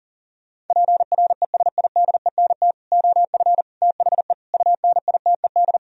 Morse code, heard as a keyed tone, about 40 wpm, 700 Hz. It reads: PRESIDENT OF THE UNITED